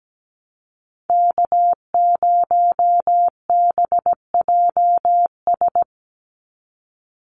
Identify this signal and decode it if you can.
Morse code, heard as a keyed tone, about 17 wpm, 700 Hz. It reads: K0BJS
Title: USA Call Signs 17wpm